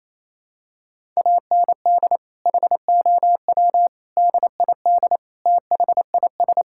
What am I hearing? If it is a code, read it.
Morse code, heard as a keyed tone, about 28 wpm, 700 Hz. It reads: AND HOW DID THIS
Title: Sentences from Top 1000 Words - Review Part 6 - 28wpm